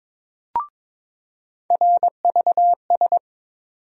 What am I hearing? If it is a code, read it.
Morse code, heard as a keyed tone, about 22 wpm, 700 Hz. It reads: RVS
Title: USA Call Sign Postfixes - 3 Letters 22wpm